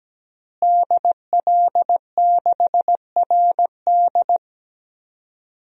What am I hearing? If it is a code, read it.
Morse code, heard as a keyed tone, about 17 wpm, 700 Hz. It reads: DL6RD
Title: DXCC-100 Call Signs - 5 Characters - Repeated 2x - 17wpm